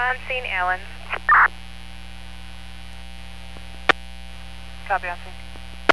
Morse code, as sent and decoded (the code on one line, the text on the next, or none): none